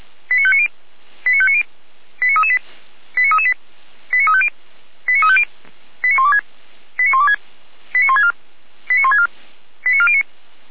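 Five-tone selective call (ZVEI), heard as quick runs of five tones, tones of about 70 ms. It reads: 89590 89590 89338 89338 89350 89350 89126 89126 89164 89164 89409
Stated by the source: ZVEI five-tone selective call received with an SDR.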